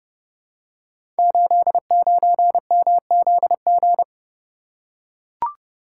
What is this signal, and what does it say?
Morse code, heard as a keyed tone, about 30 wpm, 700 Hz. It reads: 89MZG
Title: Code Groups - 5 Letter:Numbers - 30wpm